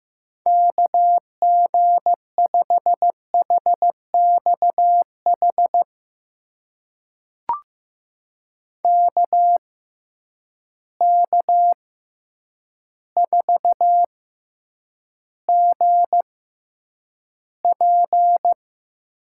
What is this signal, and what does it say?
Morse code, heard as a keyed tone, about 15 wpm, spaced as for 8 wpm, 700 Hz. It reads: KG5HXH KK4GP
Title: USA Call Signs - 6x Character Spacing - 15wpm